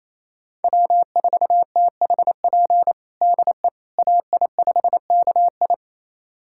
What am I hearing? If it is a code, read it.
Morse code, heard as a keyed tone, about 28 wpm, 700 Hz. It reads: W4THP DE AI5KI
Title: USA Callsign DE Callsign - 5 Characters - 28wpm